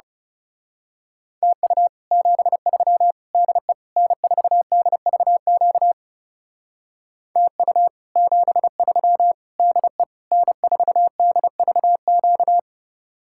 Morse code, heard as a keyed tone, about 34 wpm, 700 Hz. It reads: TU 73 DE N4DVQ TU 73 DE N4DVQ
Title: Speed-Racing Common QSO Phrases Encore - 30wpm